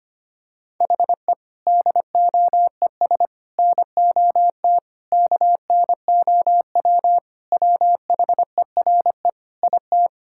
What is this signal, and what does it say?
Morse code, heard as a keyed tone, about 25 wpm, 700 Hz. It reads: HE DOES NOT KNOW WHERE IT
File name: practice / 25wpm / Sentences with 4-Word Phrases - 25wpm